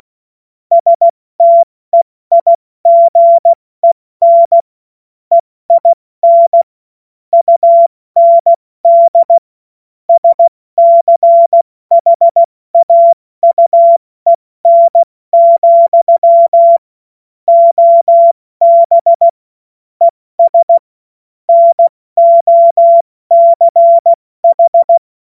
Morse code, heard as a keyed tone, about 16 wpm, 680 Hz. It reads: STEIGEN EIN UND SCHAUEN, OB ES NOCH
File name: Morsekurs / Witze / Witz 14. Dezember 2021, 70 F80